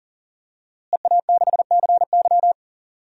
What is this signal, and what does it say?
Morse code, heard as a keyed tone, about 37 wpm, 700 Hz. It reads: EA6CY
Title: DXCC-100 Call Signs - 5 Characters - Repeated 2x - 40wpm